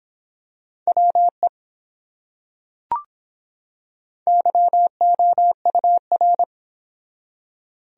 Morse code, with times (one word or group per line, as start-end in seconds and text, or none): 0.9-1.5 WE
4.3-6.4 YOUR